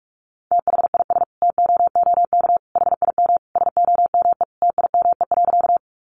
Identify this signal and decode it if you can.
Morse code, heard as a keyed tone, about 44 wpm, 700 Hz. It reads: THIS TOOK HIM SOME TIME.
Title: Sentences from Top 300 Words 45wpm